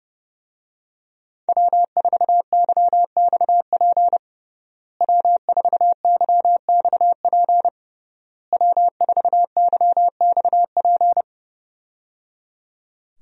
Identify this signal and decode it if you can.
Morse code, heard as a keyed tone, about 30 wpm, 700 Hz. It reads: W4YXP W4YXP W4YXP
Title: USA Call Signs - 5 Characters - Repeated 3x - 30wpm